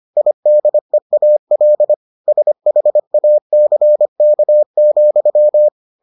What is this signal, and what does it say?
Morse code, heard as a keyed tone, about 25 wpm, 600 Hz. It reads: IDEAL SHACK,